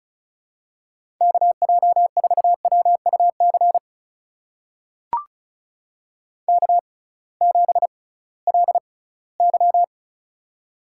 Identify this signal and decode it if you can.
Morse code, heard as a keyed tone, about 36 wpm, spaced as for 20 wpm, 700 Hz. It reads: KJ4WUC K7LY
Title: USA Call Signs - 6x Character Spacing - 35wpm